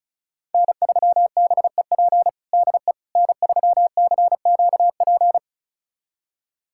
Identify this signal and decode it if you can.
Morse code, heard as a keyed tone, about 36 wpm, 700 Hz. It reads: N3BEP DE N3CQP
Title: USA Callsign DE Callsign - 5 Characters - 35wpm